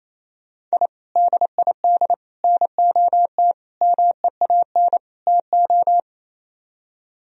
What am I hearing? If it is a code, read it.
Morse code, heard as a keyed tone, about 35 wpm, 700 Hz. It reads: I DID NOT MEAN TO
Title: Speed-Racing - Sets of 5 Words 28wpm